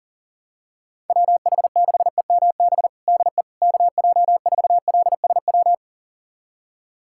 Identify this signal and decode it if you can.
Morse code, heard as a keyed tone, about 40 wpm, 700 Hz. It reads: WH6EMB DE KJ4LSW
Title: USA Callsign DE Callsign 40wpm